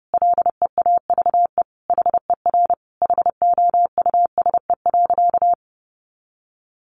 Morse code, heard as a keyed tone, about 30 wpm, 700 Hz. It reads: LEAVE HER HOUSE.
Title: Sentences from Top 1500 Words 30wpm